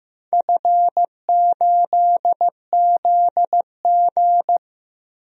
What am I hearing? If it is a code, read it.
Morse code, heard as a keyed tone, about 15 wpm, 700 Hz. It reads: F8ZG
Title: DXCC-100 Call Signs - 4 Characters - 15wpm